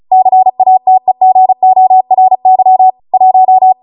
Morse code, heard as a keyed tone, about 36 wpm, 750 Hz. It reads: CATEGORY 1